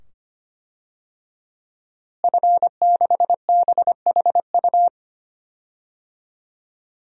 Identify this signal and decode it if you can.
Morse code, heard as a keyed tone, about 25 wpm, 700 Hz. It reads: F6BHU
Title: DXCC-100 Call Signs - 5 Characters - Repeated 2x - 25wpm